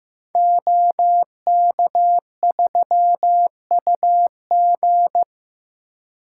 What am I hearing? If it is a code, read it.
Morse code, heard as a keyed tone, about 15 wpm, 700 Hz. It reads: OK3UG
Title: Code Groups - 5 Letter:Numbers - 15wpm